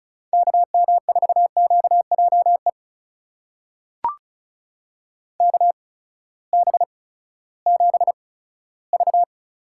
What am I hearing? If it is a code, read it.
Morse code, heard as a keyed tone, about 36 wpm, 700 Hz. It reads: K M 4 Q J E K B 7 V
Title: USA Call Signs - 6 Characters - 8x Character Spacing - 35wpm